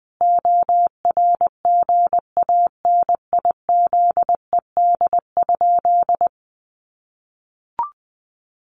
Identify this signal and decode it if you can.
Morse code, heard as a keyed tone, about 20 wpm, 700 Hz. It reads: ORGANIZED?
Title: Sentences from Top 1300 Words 20wpm